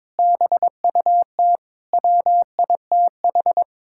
Morse code, heard as a keyed tone, about 22 wpm, 700 Hz. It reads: BUT WITH
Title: Sentences from Top 1000 Words - Review Part 5 - 22wpm